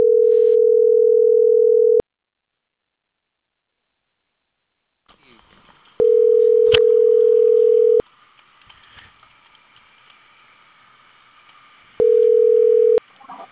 Ringback tone (North America), 440 + 480 Hz.